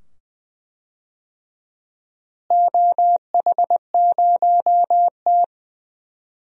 Morse code, heard as a keyed tone, about 21 wpm, 700 Hz. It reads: OH0T
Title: DXCC-100 Call Signs - 4 Characters - Repeated 2x - 20wpm